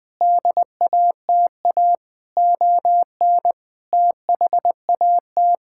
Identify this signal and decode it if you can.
Morse code, heard as a keyed tone, about 20 wpm, 700 Hz. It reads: DATA ON THAT
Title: Sentences from Top 1000 Words - Review Part 6 - 20wpm